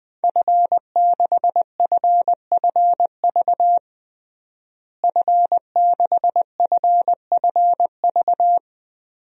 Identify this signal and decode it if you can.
Morse code, heard as a keyed tone, about 21 wpm, 700 Hz. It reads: F6FFV F6FFV
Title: DXCC-100 Call Signs - 5 Characters - Repeated 2x - 20wpm